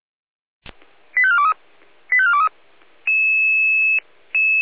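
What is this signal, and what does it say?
Five-tone selective call (ZVEI), heard as quick runs of five tones, tones of about 80 ms. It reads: 85423 85423
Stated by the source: Five-tone ZVEI selective call received via SDR, from fire brigade Feuerwehr Orlamünde's audio server.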